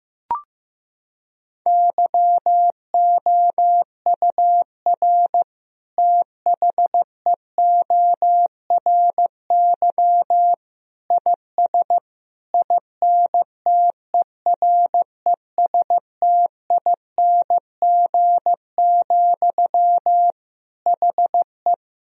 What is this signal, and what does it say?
Morse code, heard as a keyed tone, about 16 wpm, 700 Hz. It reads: YOUR THEORY IS INTERESTING, HE
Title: Sentences from Top 1000 Words - Review Part 3 - 15wpm